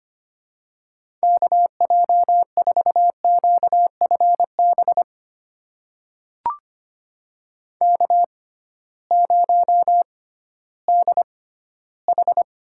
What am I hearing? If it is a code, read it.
Morse code, heard as a keyed tone, about 25 wpm, spaced as for 15 wpm, 700 Hz. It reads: KJ4QFB K0DH
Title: USA Call Signs - 6x Character Spacing - 25wpm